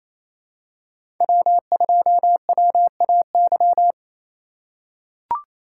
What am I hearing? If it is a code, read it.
Morse code, heard as a keyed tone, about 27 wpm, 700 Hz. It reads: W2WAY
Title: USA Call Signs - 5 Characters - 4x Character Spacing - 28wpm